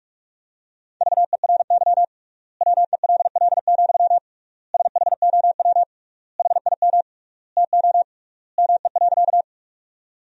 Morse code, heard as a keyed tone, about 46 wpm, 700 Hz. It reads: VERY WELL, SHOW HIM TO ME.
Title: 3x Word-Spacing - Sentences from Top 200 Words - 45wpm